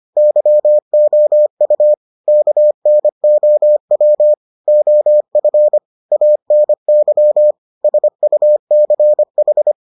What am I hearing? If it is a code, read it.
Morse code, heard as a keyed tone, about 25 wpm, 600 Hz. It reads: YOU KNOW OF ANY SUCH